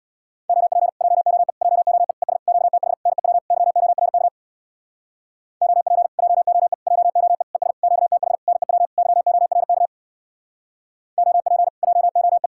Morse code, heard as a keyed tone, about 67 wpm, 700 Hz. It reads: OP JOE JOE ES QTH NEW YORK OP JOE JOE ES QTH NEW YORK OP JOE